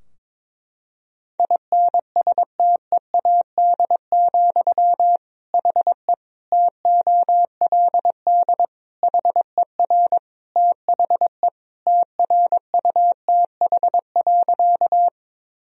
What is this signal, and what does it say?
Morse code, heard as a keyed tone, about 22 wpm, 700 Hz. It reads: INSTEAD, HE TOLD HER THE TRUTH.
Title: Sentences from Top 1000 Words - Review Part 4 - 22wpm